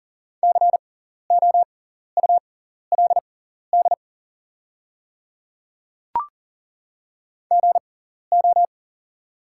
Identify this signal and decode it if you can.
Morse code, heard as a keyed tone, about 40 wpm, spaced as for 14 wpm, 700 Hz. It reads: COULD GO
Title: 6x Character Spacing - Top 100 Words - 40wpm